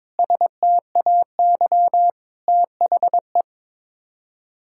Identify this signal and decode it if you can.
Morse code, heard as a keyed tone, about 21 wpm, 700 Hz. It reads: STAY THE
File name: Sets of 9 Words - 22wpm